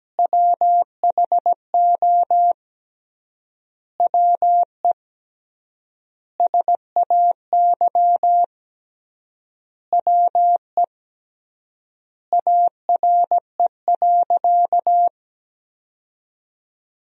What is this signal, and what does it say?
Morse code, heard as a keyed tone, about 17 wpm, 700 Hz. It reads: WHO WE SAY WE ARE.
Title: 3x Word-Spacing - Sentences from Top 200 Words - 17wpm